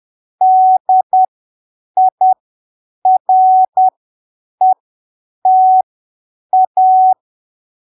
Morse code, heard as a keyed tone, about 10 wpm, spaced as for 7 wpm, 750 Hz. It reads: DIRETA
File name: arquivos / texto / texto 01 10ppm